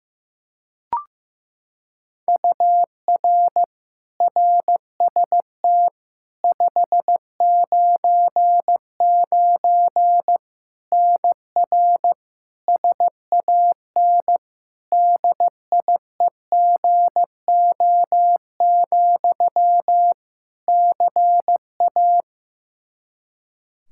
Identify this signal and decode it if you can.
Morse code, heard as a keyed tone, about 15 wpm, 700 Hz. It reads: UR RST 599 NR SAN DIEGO, CA
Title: Practice QSO - Exchange 1 - Beginning - 15wpm